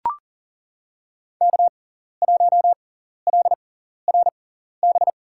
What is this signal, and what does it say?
Morse code, heard as a keyed tone, about 40 wpm, spaced as for 14 wpm, 700 Hz. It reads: K1LRB
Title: USA Call Signs - 5 Characters - 6x Character Spacing - 40wpm